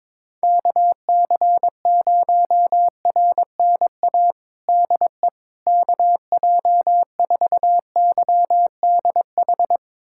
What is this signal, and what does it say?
Morse code, heard as a keyed tone, about 22 wpm, 700 Hz. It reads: KC0RNA DE KJ4YDH
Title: USA Callsign DE Callsign 22wpm